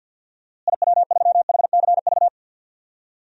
Morse code, heard as a keyed tone, about 50 wpm, 700 Hz. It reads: IW3HCV
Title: DXCC-100 Call Signs - 6 Characters - Repeated 3x - 50wpm